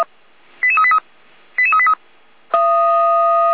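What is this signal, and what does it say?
Five-tone selective call (ZVEI), heard as quick runs of five tones, tones of about 70 ms. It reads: 88383 88383
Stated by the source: ZVEI five-tone selective call received on SDR.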